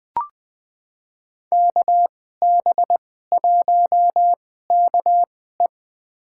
Morse code, heard as a keyed tone, about 20 wpm, spaced as for 14 wpm, 700 Hz. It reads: KB1KE
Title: USA Call Signs - 5 Characters - 2x Character Spacing - 20wpm